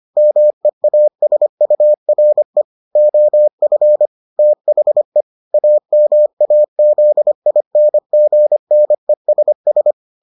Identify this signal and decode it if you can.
Morse code, heard as a keyed tone, about 25 wpm, 600 Hz. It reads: MEASURE OF THE AMAZINGNESS